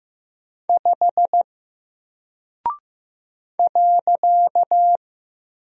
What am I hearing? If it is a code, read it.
Morse code, heard as a keyed tone, about 15 wpm, 700 Hz. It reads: 5 .
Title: Single Character (Letter:Number:Punctuation:Prosign) 15wpm